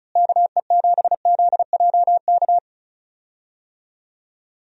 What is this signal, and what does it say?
Morse code, heard as a keyed tone, about 34 wpm, 700 Hz. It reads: KE7ZJK
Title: USA Call Signs - 6 Characters - Repeated 2x - 35wpm